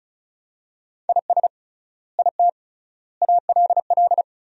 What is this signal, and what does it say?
Morse code, heard as a keyed tone, about 36 wpm, 700 Hz. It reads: IS IT ALL
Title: 3x Word-Spacing - Sentences from Top 200 Words - 35wpm